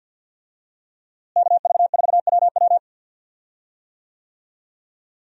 Morse code, heard as a keyed tone, about 50 wpm, 700 Hz. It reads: KV4WW
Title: USA Call Signs - 5 Characters - Repeated 2x - 50wpm